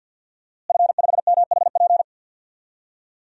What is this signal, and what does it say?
Morse code, heard as a keyed tone, about 51 wpm, 700 Hz. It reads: F5MHP
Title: DXCC-100 Call Signs - 5 Characters - Repeated 2x - 50wpm